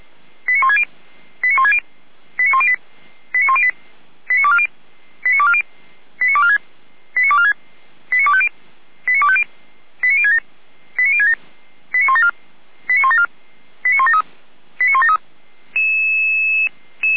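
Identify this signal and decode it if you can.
Five-tone selective call (ZVEI), heard as quick runs of five tones, tones of about 70 ms. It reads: 89160 89160 89108 89108 89240 89240 89246 89246 89250 89250 89067 89067 89174 89174 89173 89173
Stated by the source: ZVEI five-tone selective call 89336 received on SDR.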